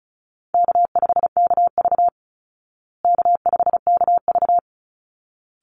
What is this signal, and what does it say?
Morse code, heard as a keyed tone, about 34 wpm, 700 Hz. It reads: K5KV K5KV
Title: Easy Call Signs Repeated 2x - 35wpm